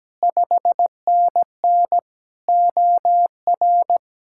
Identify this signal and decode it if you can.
Morse code, heard as a keyed tone, about 17 wpm, 700 Hz. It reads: 5NN OR
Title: Speed-Racing Common QSO Phrases - 17wpm